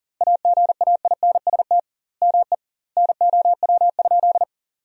Morse code, heard as a keyed tone, about 40 wpm, 700 Hz. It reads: AGAINST ME NOW?
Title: 2x Word-Spacing - Sentences from Top 400 Words - 40wpm